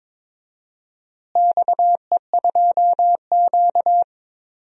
Morse code, heard as a keyed tone, about 22 wpm, 700 Hz. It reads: XE2Q
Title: DXCC-100 Call Signs - 4 Characters - Repeated 3x - 22wpm